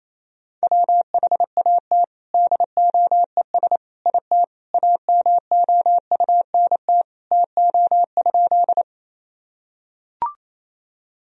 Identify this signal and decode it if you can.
Morse code, heard as a keyed tone, about 28 wpm, 700 Hz. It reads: WHAT DOES IT AMOUNT TO?
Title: Sentences from Top 1000 Words - Review Part 4 - 28wpm